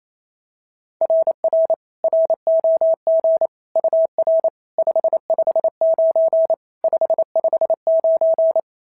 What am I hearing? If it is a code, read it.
Morse code, heard as a keyed tone, about 28 wpm, 690 Hz. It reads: RR ROG UR 559 559
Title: SOTA QSO Practice 28wpm